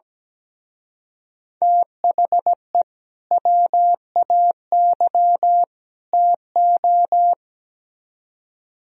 Morse code, heard as a keyed tone, about 20 wpm, 700 Hz. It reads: THE WAY TO
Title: Speed-Racing - Sets of 3 Words 17wpm